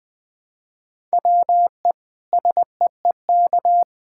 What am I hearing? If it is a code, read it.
Morse code, heard as a keyed tone, about 20 wpm, 700 Hz. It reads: WE SEEK
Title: Sentences from Top 1600 Words 20wpm